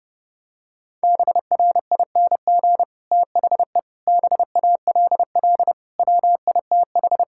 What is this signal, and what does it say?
Morse code, heard as a keyed tone, about 30 wpm, 700 Hz. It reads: BRING THE BALL WITH
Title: Sentences from Top 1000 Words - Review Part 6 - 30wpm